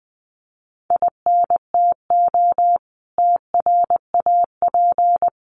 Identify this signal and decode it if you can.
Morse code, heard as a keyed tone, about 20 wpm, 700 Hz. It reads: INTO TRAP
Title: Common QSO Phrases 20wpm